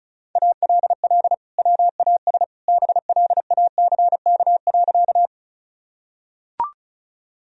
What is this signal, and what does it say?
Morse code, heard as a keyed tone, about 34 wpm, 700 Hz. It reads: ALL WAS BLACK.
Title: Sentences from Top 1000 Words - Review Part 2 - 35wpm